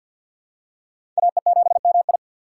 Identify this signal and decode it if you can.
Morse code, heard as a keyed tone, about 51 wpm, 700 Hz. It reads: AE7MI
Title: USA Call Signs - 5 Characters - 2x Character Spacing - 50wpm